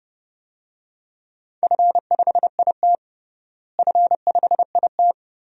Morse code, heard as a keyed tone, about 30 wpm, 700 Hz. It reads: F5IT F5IT
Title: DXCC-100 Call Signs - 4 Characters - Repeated 2x - 30wpm